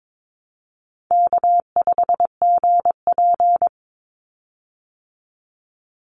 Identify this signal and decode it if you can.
Morse code, heard as a keyed tone, about 22 wpm, 700 Hz. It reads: K5GP
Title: Easy Call Signs Repeated 2x - 22wpm